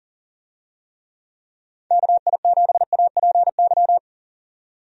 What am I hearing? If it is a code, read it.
Morse code, heard as a keyed tone, about 40 wpm, 700 Hz. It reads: KI7APY
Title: USA Call Signs - 6 Characters - 40wpm